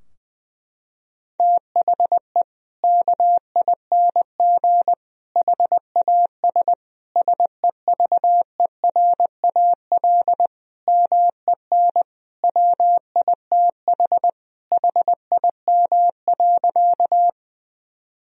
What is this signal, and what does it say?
Morse code, heard as a keyed tone, about 21 wpm, 700 Hz. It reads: THE KING HAS SEVERAL MEN WITH HIM.
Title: Sentences from Top 1000 Words - Review Part 2 - 20wpm